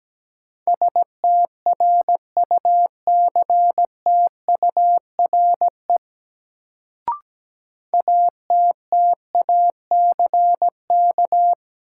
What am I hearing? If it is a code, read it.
Morse code, heard as a keyed tone, about 17 wpm, 700 Hz. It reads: STRUCTURE ATTACK
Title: Sentences from Top 1500 Words - New Word Drill Alt - 17wpm